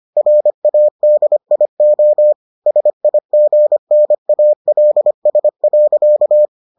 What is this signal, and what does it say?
Morse code, heard as a keyed tone, about 25 wpm, 600 Hz. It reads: RADIO SIGNALS.